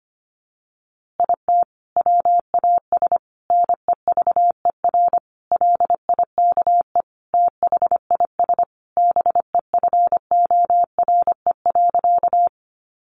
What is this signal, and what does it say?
Morse code, heard as a keyed tone, about 25 wpm, 700 Hz. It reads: IT WAS NEVER LIKE THIS BEFORE.
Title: A Sound Of Thunder 25wpm